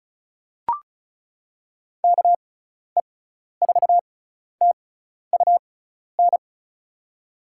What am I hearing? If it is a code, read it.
Morse code, heard as a keyed tone, about 36 wpm, spaced as for 12 wpm, 700 Hz. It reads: KE4TUN
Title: USA Call Signs - 6 Characters - 6x Character Spacing - 35wpm